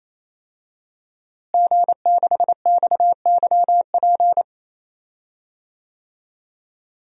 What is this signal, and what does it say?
Morse code, heard as a keyed tone, about 27 wpm, 700 Hz. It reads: G6XYP
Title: DXCC-100 Call Signs - 5 Characters - Repeated 3x - 28wpm